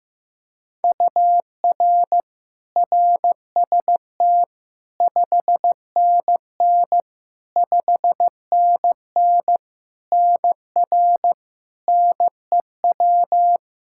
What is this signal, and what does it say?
Morse code, heard as a keyed tone, about 15 wpm, 700 Hz. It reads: UR RST 5NN 5NN NR NEW
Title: K1USN SST QSO Practice 15wpm